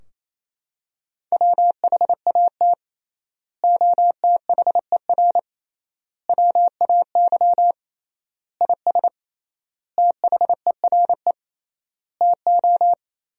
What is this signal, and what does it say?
Morse code, heard as a keyed tone, about 28 wpm, 700 Hz. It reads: WHAT OTHER WAY IS THERE TO
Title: 3x Word-Spacing - Sentences from Top 100 Words - 28wpm